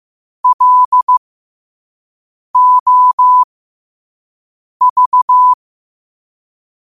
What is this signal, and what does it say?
Morse code, heard as a keyed tone, about 15 wpm, spaced as for 5 wpm, 1000 Hz. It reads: LOV
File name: 5wpm